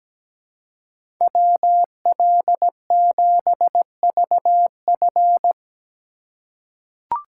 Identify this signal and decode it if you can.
Morse code, heard as a keyed tone, about 17 wpm, 700 Hz. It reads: WL7VF
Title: USA Call Signs - 5 Characters - 2x Character Spacing - 17wpm